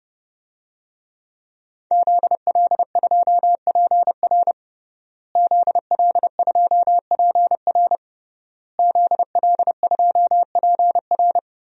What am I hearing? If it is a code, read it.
Morse code, heard as a keyed tone, about 30 wpm, 700 Hz. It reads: ZL2PR ZL2PR ZL2PR
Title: DXCC-100 Call Signs - 5 Characters - Repeated 3x - 30wpm